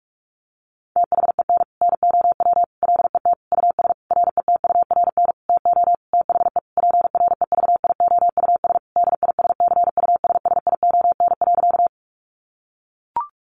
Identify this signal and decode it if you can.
Morse code, heard as a keyed tone, about 44 wpm, 700 Hz. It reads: THEN NOW LET US RETURN TO THE PREVIOUS DISCUSSION.
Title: Sentences from Top 1500 Words 45wpm